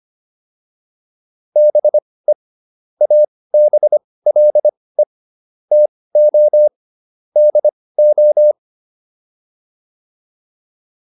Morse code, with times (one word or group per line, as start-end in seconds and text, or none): 1.6-2.3 BE
3.0-5.0 ABLE
5.7-6.7 TO
7.4-8.5 DO